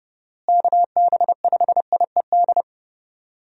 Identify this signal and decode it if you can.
Morse code, heard as a keyed tone, about 30 wpm, 700 Hz. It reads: KB5IED
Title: USA Call Signs - 6 Characters - Repeated 2x - 30wpm